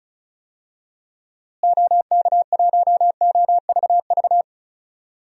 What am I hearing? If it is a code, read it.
Morse code, heard as a keyed tone, about 36 wpm, 700 Hz. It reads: OK1OVV
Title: DXCC-100 Call Signs - 6 Characters - Repeated 2x - 35wpm